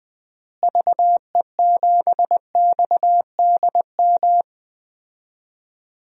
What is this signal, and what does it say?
Morse code, heard as a keyed tone, about 21 wpm, 700 Hz. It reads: VE7XDM
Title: DXCC-100 Call Signs - 6 Characters - Repeated 3x - 20wpm